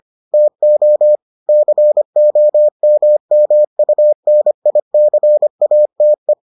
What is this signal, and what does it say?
Morse code, heard as a keyed tone, about 25 wpm, 600 Hz. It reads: TO COMMUNICATE